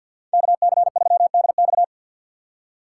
Morse code, heard as a keyed tone, about 52 wpm, 700 Hz. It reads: KC3DX
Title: USA Call Signs - 5 Characters - 4x Character Spacing - 50wpm